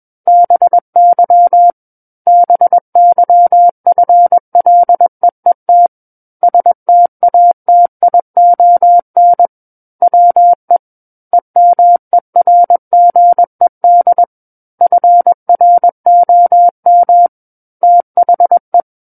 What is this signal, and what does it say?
Morse code, heard as a keyed tone, about 21 wpm, 700 Hz. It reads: BY BYFLEET STATION WE EMERGED FROM THE
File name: Book One / 12-Chapter 12 - 21 WPM